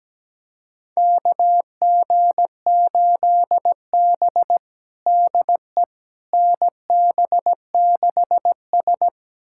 Morse code, heard as a keyed tone, about 17 wpm, 700 Hz. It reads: KG8B DE NB6S
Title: USA Callsign DE Callsign - 4 Characters - 17wpm